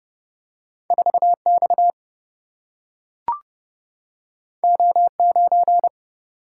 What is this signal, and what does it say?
Morse code, heard as a keyed tone, about 30 wpm, 700 Hz. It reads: 4X O9
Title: Single-Letter Single-Number 30wpm